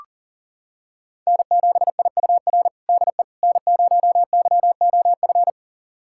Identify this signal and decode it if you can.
Morse code, heard as a keyed tone, about 40 wpm, 700 Hz. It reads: N7IUR DE N0YOF
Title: USA Callsign DE Callsign - 5 Characters - 40wpm